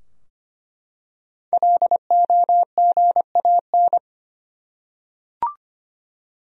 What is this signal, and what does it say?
Morse code, heard as a keyed tone, about 25 wpm, 700 Hz. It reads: LOGAN
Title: USA Common Names - 25wpm